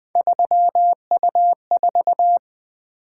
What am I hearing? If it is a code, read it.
Morse code, heard as a keyed tone, about 20 wpm, 700 Hz. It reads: 3U4
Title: Three Letter-Number Combinations - 20wpm